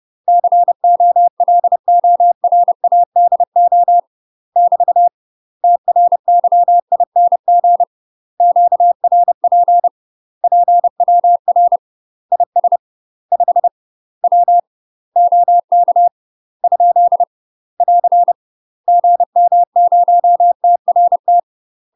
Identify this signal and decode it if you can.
Morse code, heard as a keyed tone, about 30 wpm, 700 Hz. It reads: COLORADO <BT> TRYING QRP PWR IS 5 W OK ? <AR> GM0TRT
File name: UK MASTER QSOs TRAINING / 30WPM / UK MASTER QSOs 0051